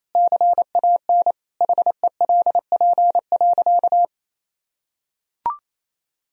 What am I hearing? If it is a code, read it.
Morse code, heard as a keyed tone, about 28 wpm, 700 Hz. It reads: CAN HELP.
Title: Sentences from Top 1000 Words - Review Part 4 - 28wpm